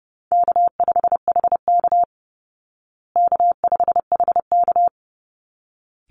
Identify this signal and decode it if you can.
Morse code, heard as a keyed tone, about 30 wpm, 700 Hz. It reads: K5HK K5HK
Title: Easy Call Signs Repeated 2x - 30wpm